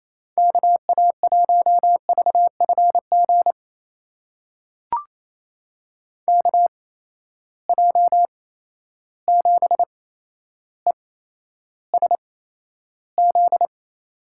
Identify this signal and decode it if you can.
Morse code, heard as a keyed tone, about 28 wpm, spaced as for 12 wpm, 700 Hz. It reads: KA1VFG KJ7ESZ
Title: USA Call Signs - 6 Characters - 8x Character Spacing - 28wpm